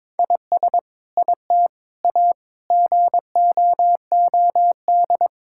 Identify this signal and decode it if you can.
Morse code, heard as a keyed tone, about 22 wpm, 700 Hz. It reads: IS IT A GOOD
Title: Sentences from Top 1000 Words - Review Part 5 - 22wpm